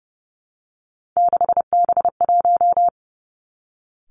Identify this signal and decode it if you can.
Morse code, heard as a keyed tone, about 30 wpm, 700 Hz. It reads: 6B1
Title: Three Challenging Letter-Number Combinations 30wpm